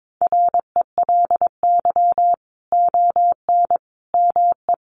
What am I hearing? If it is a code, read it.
Morse code, heard as a keyed tone, about 22 wpm, 700 Hz. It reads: RELY ON ME
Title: Sentences from Top 1200 Words 22wpm